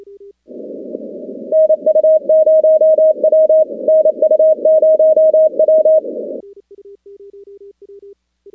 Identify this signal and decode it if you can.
Morse code, heard as a keyed tone, about 27 wpm, 630 Hz. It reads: NU0W NU0W
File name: contests / 2019-cw-wpx-kt5j-k5pi- -2330 0-16